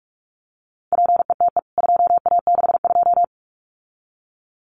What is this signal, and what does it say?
Morse code, heard as a keyed tone, about 44 wpm, 700 Hz. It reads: PETE 2A62
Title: CWT Exchange 45wpm